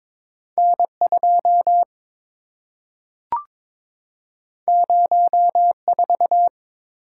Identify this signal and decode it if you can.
Morse code, heard as a keyed tone, about 22 wpm, 700 Hz. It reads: N2 04
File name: practice / 22wpm / Two Letter-Number Combinations - 22wpm